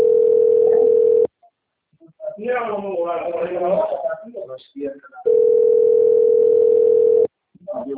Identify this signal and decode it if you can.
Ringback tone (North America), 440 + 480 Hz, with speech.
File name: out- - - - - .35778